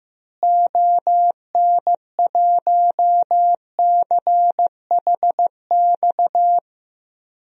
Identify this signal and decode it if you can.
Morse code, heard as a keyed tone, about 15 wpm, 700 Hz. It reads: ON1CHX
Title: DXCC-100 Call Signs - 6 Characters - Repeated 2x - 15wpm